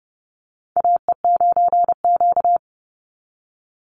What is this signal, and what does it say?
Morse code, heard as a keyed tone, about 30 wpm, 700 Hz. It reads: AE9Q
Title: NA SOTA Chaser Callsigns 30wpm